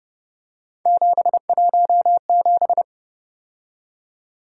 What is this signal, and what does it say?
Morse code, heard as a keyed tone, about 29 wpm, 700 Hz. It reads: 717
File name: Speed-Racing - Three Letter-Number Combinations - 25wpm